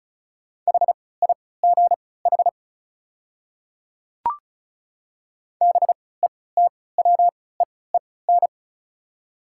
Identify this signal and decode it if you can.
Morse code, heard as a keyed tone, about 36 wpm, spaced as for 20 wpm, 700 Hz. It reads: HIGH BETWEEN
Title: 3x Character Spacing - Top 300 Words - 35wpm